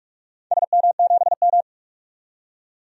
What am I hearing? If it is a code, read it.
Morse code, heard as a keyed tone, about 46 wpm, 700 Hz. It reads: SM7M
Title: DXCC-100 Call Signs - 4 Characters - 45wpm